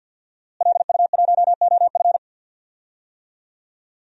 Morse code, heard as a keyed tone, about 52 wpm, 700 Hz. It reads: LU1OF
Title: DXCC-100 Call Signs - 5 Characters - Repeated 3x - 50wpm